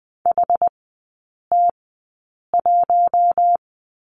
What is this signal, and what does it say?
Morse code, heard as a keyed tone, about 20 wpm, spaced as for 8 wpm, 700 Hz. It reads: HT1
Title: 107 Lesson 18e - Letter m - Three Characters 20wpm